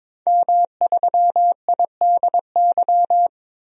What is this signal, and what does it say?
Morse code, heard as a keyed tone, about 22 wpm, 700 Hz. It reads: M3IDY
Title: DXCC-100 Call Signs - 5 Characters - 22wpm